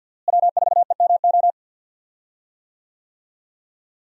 Morse code, heard as a keyed tone, about 53 wpm, 700 Hz. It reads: W4EMO
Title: USA Call Signs - 5 Characters - Repeated 2x - 50wpm